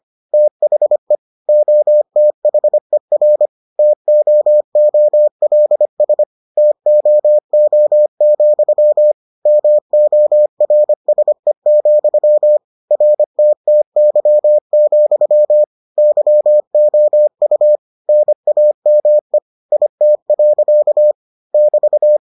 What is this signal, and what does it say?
Morse code, heard as a keyed tone, about 25 wpm, 600 Hz. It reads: THE OTHER TOOLS TOO, MORSE, RTTY, YOU NAME IT. <BT>